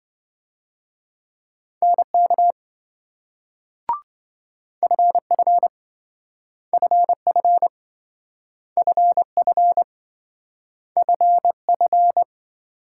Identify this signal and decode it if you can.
Morse code, heard as a keyed tone, about 27 wpm, 700 Hz. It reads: NK FF FF FF FF